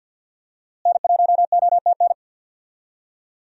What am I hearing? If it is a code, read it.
Morse code, heard as a keyed tone, about 52 wpm, 700 Hz. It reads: N1OTN
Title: USA Call Signs - 5 Characters - Repeated 2x - 50wpm